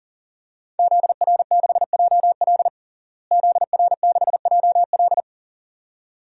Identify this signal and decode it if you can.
Morse code, heard as a keyed tone, about 40 wpm, 700 Hz. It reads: ZR6JL ZR6JL
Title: DXCC-100 Call Signs - 5 Characters - Repeated 2x - 40wpm